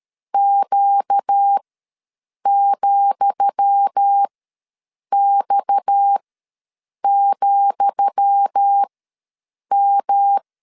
Morse code, heard as a keyed tone, about 13 wpm, spaced as for 7 wpm, 800 Hz. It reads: Q,X,M
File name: LEZIONI / 10 SIGNS / 10D COMA ws 12wpm 3618